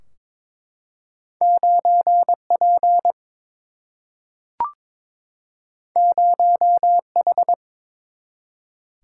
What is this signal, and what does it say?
Morse code, heard as a keyed tone, about 22 wpm, 700 Hz. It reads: 9P 0H
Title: Single-Letter Single-Number 22wpm